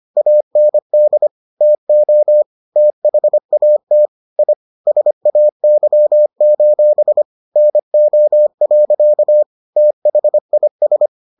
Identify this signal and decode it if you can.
Morse code, heard as a keyed tone, about 25 wpm, 600 Hz. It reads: AND TO THAT I SAY: NO. THIS